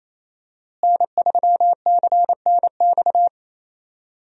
Morse code, heard as a keyed tone, about 28 wpm, 700 Hz. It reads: N3CNX
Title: USA Call Signs - 5 Characters - Repeated 2x - 28wpm